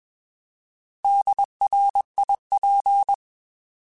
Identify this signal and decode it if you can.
Morse code, heard as a keyed tone, about 21 wpm, 800 Hz. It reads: DRIP
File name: Alphabetical / Alphabetical mp3s / Ds @20